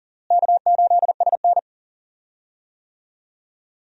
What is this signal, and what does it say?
Morse code, heard as a keyed tone, about 39 wpm, 700 Hz. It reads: K8SN
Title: USA Call Signs - 4 Characters - Repeated 2x - 40wpm